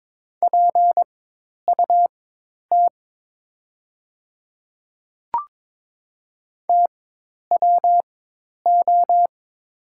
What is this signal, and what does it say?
Morse code, heard as a keyed tone, about 22 wpm, spaced as for 10 wpm, 700 Hz. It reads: PUT TWO
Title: 4x Character Spacing - Top 200 Words - 22wpm